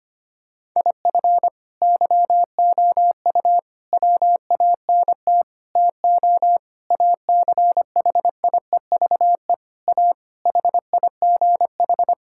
Morse code, heard as a keyed tone, about 25 wpm, 700 Hz. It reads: IF YOU WANT TO ACHIEVE A HIGH
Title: Sentences from Top 1000 Words - Review Part 6 - 25wpm